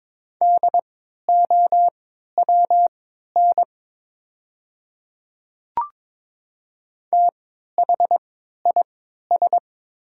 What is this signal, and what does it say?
Morse code, heard as a keyed tone, about 22 wpm, spaced as for 12 wpm, 700 Hz. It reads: DOWN THIS
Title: 3x Character Spacing - Top 200 Words - 22wpm